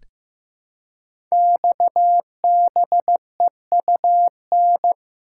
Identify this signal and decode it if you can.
Morse code, heard as a keyed tone, about 15 wpm, 700 Hz. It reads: XBEUN
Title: Code Groups - 5 Letters - 15wpm